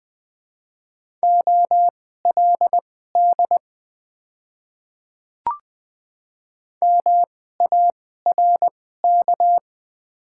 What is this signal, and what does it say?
Morse code, heard as a keyed tone, about 20 wpm, spaced as for 14 wpm, 700 Hz. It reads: OLD MARK
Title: 2x Character Spacing - Top 300 Words - 20wpm